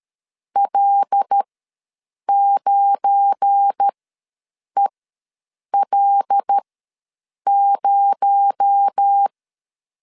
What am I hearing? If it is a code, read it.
Morse code, heard as a keyed tone, about 13 wpm, spaced as for 7 wpm, 800 Hz. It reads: L9EL0
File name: LEZIONI / 7 GF9Z / 7 GF9 ws 12wpm 3344